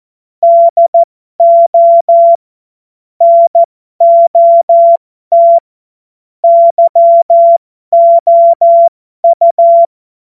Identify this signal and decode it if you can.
Morse code, heard as a keyed tone, about 14 wpm, spaced as for 12 wpm, 680 Hz. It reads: DO NOT YOU
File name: Morsekurs / Witze